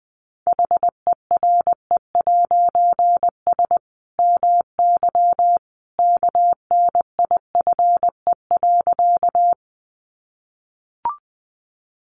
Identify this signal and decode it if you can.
Morse code, heard as a keyed tone, about 20 wpm, 700 Hz. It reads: HERE'S MY KNIFE.
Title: A Sound Of Thunder 20wpm